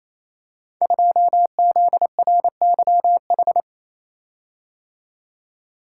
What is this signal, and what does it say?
Morse code, heard as a keyed tone, about 27 wpm, 700 Hz. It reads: 2ZRYH
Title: Code Groups - 5 Letter:Numbers - 28wpm